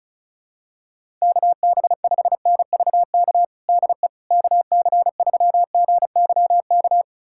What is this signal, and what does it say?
Morse code, heard as a keyed tone, about 34 wpm, 700 Hz. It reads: KB5NVK DE KC3GYK
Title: USA Callsign DE Callsign 35wpm